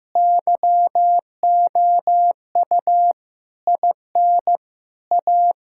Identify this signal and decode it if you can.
Morse code, heard as a keyed tone, about 15 wpm, 700 Hz. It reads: YOU IN A
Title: Sentences from Top 1000 Words - Review Part 6 - 15wpm